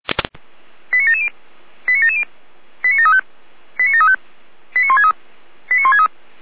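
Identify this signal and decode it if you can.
Five-tone selective call (ZVEI), heard as quick runs of five tones, tones of about 70 ms. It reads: 89770 89770 89735 89735 89163 89163
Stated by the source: ZVEI five-tone selective call received on an SDR.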